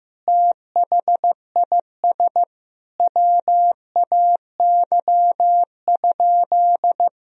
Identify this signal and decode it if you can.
Morse code, heard as a keyed tone, about 15 wpm, 700 Hz. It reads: THIS WAY?
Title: Sentences from Top 1000 Words - Review Part 1 - 15wpm